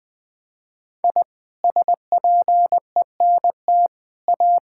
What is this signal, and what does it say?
Morse code, heard as a keyed tone, about 20 wpm, 700 Hz. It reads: I SPENT A
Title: Sets of 6 Words - 20wpm